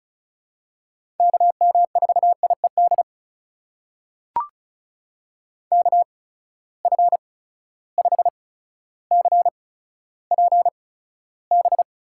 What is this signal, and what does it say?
Morse code, heard as a keyed tone, about 36 wpm, spaced as for 15 wpm, 700 Hz. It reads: KM4IED KF5CPB
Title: USA Call Signs - 6 Characters - 8x Character Spacing - 35wpm